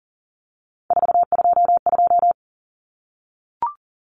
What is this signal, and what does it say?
Morse code, heard as a keyed tone, about 40 wpm, 700 Hz. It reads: 422
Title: Numbers - 3-Digit 40wpm